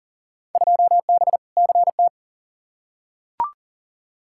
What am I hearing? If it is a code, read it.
Morse code, heard as a keyed tone, about 38 wpm, 700 Hz. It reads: 2B CT
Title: Field Day Exchanges Encore - 40wpm